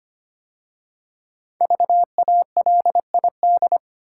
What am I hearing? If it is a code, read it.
Morse code, heard as a keyed tone, about 25 wpm, 700 Hz. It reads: VALID
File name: 5 Letter Words - First 3 Letters Given - 25wpm